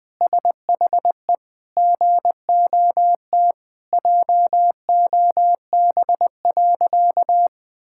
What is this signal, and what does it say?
Morse code, heard as a keyed tone, about 20 wpm, 700 Hz. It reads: SHE GOT JOB.
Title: Sentences with 4-Word Phrases - Spoken Beginning - 20wpm